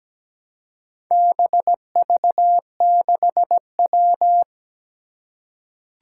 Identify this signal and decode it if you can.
Morse code, heard as a keyed tone, about 17 wpm, 700 Hz. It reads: BV6W
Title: Four Challenging Letter-Number Combinations 17wpm